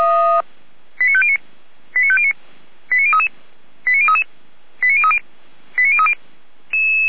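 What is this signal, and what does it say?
Five-tone selective call (ZVEI), heard as quick runs of five tones, tones of about 80 ms. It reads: 89509 89509 89033 89033 89030 89030
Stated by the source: Five-tone ZVEI call received by SDR, from the feuerwehr-orlamuende.org fire brigade audio server.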